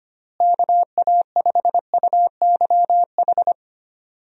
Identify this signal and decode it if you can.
Morse code, heard as a keyed tone, about 25 wpm, 700 Hz. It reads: KA5UYH
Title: USA Call Signs - 6 Characters - Repeated 3x - 25wpm